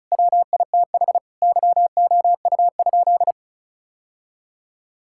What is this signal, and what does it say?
Morse code, heard as a keyed tone, about 34 wpm, 700 Hz. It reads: WITH YOU?
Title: Sentences from Top 1000 Words - Review Part 5 - 35wpm